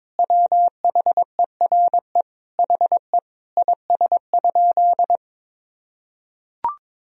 Morse code, heard as a keyed tone, about 22 wpm, 700 Hz. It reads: WHERE HE IS?
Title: Sentences from Top 300 Words and Abbreviations 22wpm